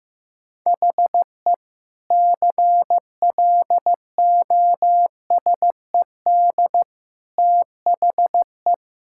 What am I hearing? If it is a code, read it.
Morse code, heard as a keyed tone, about 16 wpm, 700 Hz. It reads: HE CLOSED THE
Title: Sentences from Top 1600 Words 15wpm